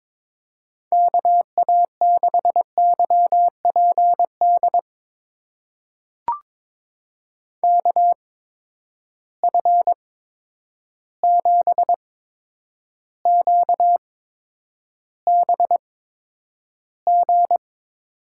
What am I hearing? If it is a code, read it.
Morse code, heard as a keyed tone, about 22 wpm, spaced as for 9 wpm, 700 Hz. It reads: KA6YPD KF7QBG
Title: USA Call Signs - 6 Characters - 8x Character Spacing - 22wpm